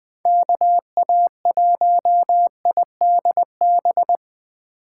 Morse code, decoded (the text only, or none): KA1IDB